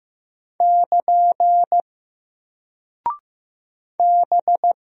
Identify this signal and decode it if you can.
Morse code, heard as a keyed tone, about 15 wpm, 700 Hz. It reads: <KN> B
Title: Single Character (Letter:Number:Punctuation:Prosign) Alt 15wpm